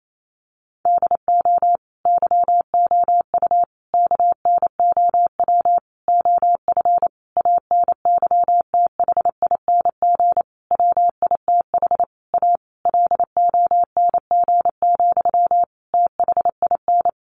Morse code, heard as a keyed tone, about 28 wpm, 700 Hz. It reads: DO YOU KNOW OF ANYTHING WITH A LONG, THIN
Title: Sentences from Top 1200 Words 28wpm